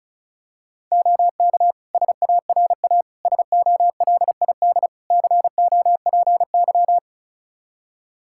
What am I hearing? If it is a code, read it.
Morse code, heard as a keyed tone, about 36 wpm, 700 Hz. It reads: OK SARA SOLID COPY
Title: Practice QSO - Exchange 3 - More Info - 35wpm